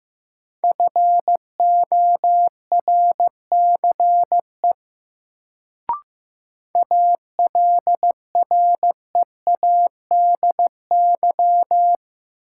Morse code, heard as a keyed tone, about 15 wpm, 700 Hz. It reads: FORCE ALREADY
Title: Sentences from Top 700 Words - New Word Drill - 15wpm